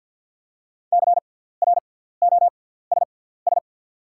Morse code, heard as a keyed tone, about 48 wpm, spaced as for 17 wpm, 690 Hz. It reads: CROSS
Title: 6x Character Spacing - Top 500 Words - 50wpm